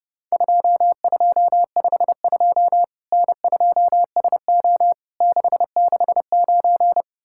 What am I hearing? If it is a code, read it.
Morse code, heard as a keyed tone, about 30 wpm, 700 Hz. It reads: 2252 N2SO 669
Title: Worked All Europe DX Contest - QTC Exchange - 30wpm